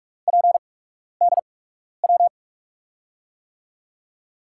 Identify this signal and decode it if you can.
Morse code, heard as a keyed tone, about 44 wpm, spaced as for 12 wpm, 700 Hz. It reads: PDW
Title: USA Call Signs - 5 Characters - 8x Character Spacing - 45wpm